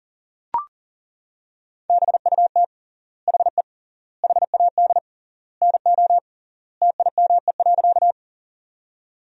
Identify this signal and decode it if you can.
Morse code, heard as a keyed tone, about 40 wpm, 700 Hz. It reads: BUT HE HAD NO TIME.
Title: 3x Word-Spacing - Sentences from Top 100 Words - 40wpm